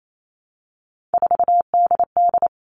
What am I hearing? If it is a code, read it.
Morse code, heard as a keyed tone, about 28 wpm, 700 Hz. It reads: 4DD
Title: All Cut Numbers - 3-Digit 28wpm